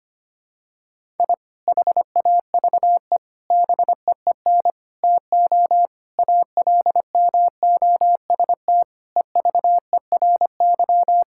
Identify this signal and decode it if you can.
Morse code, heard as a keyed tone, about 25 wpm, 700 Hz. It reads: I HAVE BEEN TO ALMOST EVERY
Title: Sentences from Top 1000 Words - Review Part 5 - 25wpm